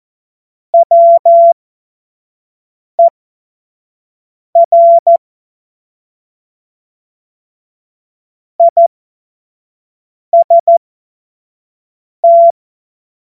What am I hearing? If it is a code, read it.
Morse code, heard as a keyed tone, about 14 wpm, spaced as for 5 wpm, 680 Hz. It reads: WER IST
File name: Morsekurs / Witze / Witz 20. Juni 2023 25 F70